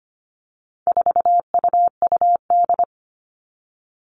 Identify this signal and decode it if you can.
Morse code, heard as a keyed tone, about 25 wpm, 700 Hz. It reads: 4UUD
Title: All Cut Numbers - 4-Digit 25wpm